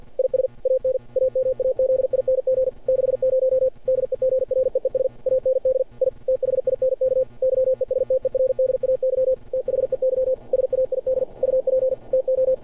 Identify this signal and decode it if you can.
Morse code, heard as a keyed tone, about 45 wpm, 530 Hz. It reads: IS AN AMAZING 60 DEGREES AND I THINK YESTERDAY THEY SAID UP TO
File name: archived / CRSnet